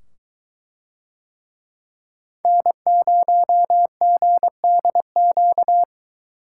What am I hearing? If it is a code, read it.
Morse code, heard as a keyed tone, about 21 wpm, 700 Hz. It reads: N0GDQ